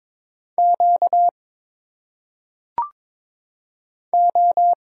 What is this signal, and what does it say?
Morse code, heard as a keyed tone, about 22 wpm, 700 Hz. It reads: Q O
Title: Single Letters - Phonetic Pronunciation 22wpm